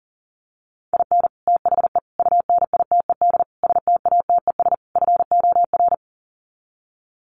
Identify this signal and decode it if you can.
Morse code, heard as a keyed tone, about 40 wpm, 700 Hz. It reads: IN THE UNITED STATES FOR
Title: Sets of 5 Words Encore2 40wpm